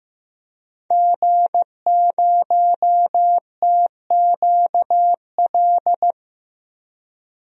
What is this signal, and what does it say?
Morse code, heard as a keyed tone, about 15 wpm, 700 Hz. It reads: G0TQL
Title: DXCC-100 Call Signs - 5 Characters - 15wpm